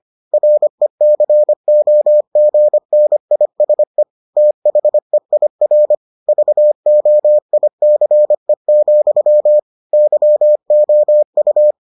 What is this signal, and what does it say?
Morse code, heard as a keyed tone, about 25 wpm, 600 Hz. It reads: RECOGNISE THEIR VOICE, YOU